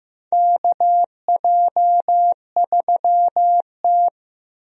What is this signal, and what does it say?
Morse code, heard as a keyed tone, about 15 wpm, 700 Hz. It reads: KJ3T
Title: USA Call Signs - 4 Characters - Repeated 3x - 15wpm